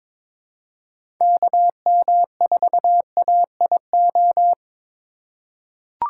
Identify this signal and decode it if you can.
Morse code, heard as a keyed tone, about 22 wpm, 700 Hz. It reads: KM4AIO
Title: USA Call Signs - 6 Characters - 3x Character Spacing - 22wpm